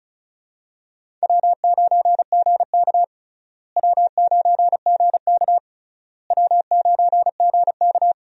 Morse code, heard as a keyed tone, about 34 wpm, 700 Hz. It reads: W9GK W9GK W9GK
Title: USA Call Signs - 4 Characters - Repeated 3x - 35wpm